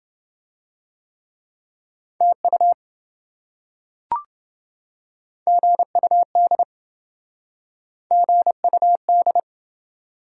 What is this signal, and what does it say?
Morse code, heard as a keyed tone, about 27 wpm, 700 Hz. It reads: TU GUD GUD